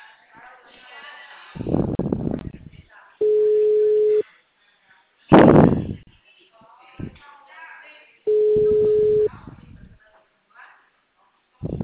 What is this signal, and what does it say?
Ringback tone (Europe, 425 Hz), with speech.